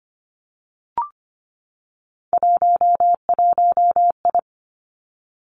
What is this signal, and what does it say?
Morse code, heard as a keyed tone, about 25 wpm, 700 Hz. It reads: 11I
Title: Three Challenging Letter-Number Combinations 25wpm